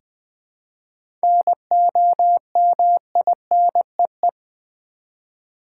Morse code, heard as a keyed tone, about 20 wpm, 690 Hz. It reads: NOMINEE
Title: 7 Letter Words - First 3 Letters Given - 20wpm